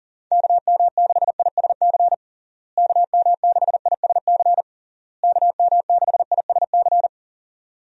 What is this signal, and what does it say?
Morse code, heard as a keyed tone, about 40 wpm, 700 Hz. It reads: KM6ISC KM6ISC KM6ISC
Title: USA Call Signs - 6 Characters - Repeated 3x - 40wpm